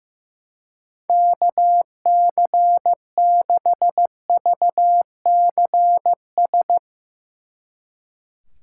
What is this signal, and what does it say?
Morse code, heard as a keyed tone, about 16 wpm, 700 Hz. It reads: KC6VCS
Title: Speed Racing - USA Call Signs 15wpm